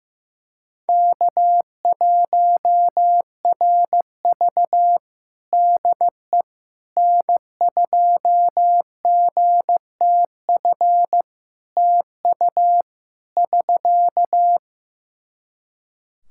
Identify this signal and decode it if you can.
Morse code, heard as a keyed tone, about 15 wpm, 700 Hz. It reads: K1RV DE N2GTF TU <SK>
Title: Practice QSO - Exchange 4 - Ending - 15wpm